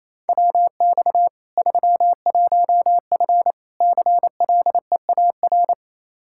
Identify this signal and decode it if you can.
Morse code, heard as a keyed tone, about 28 wpm, 700 Hz. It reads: WX 31F CLEAR
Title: Practice QSO - Exchange 2 - Info - 28wpm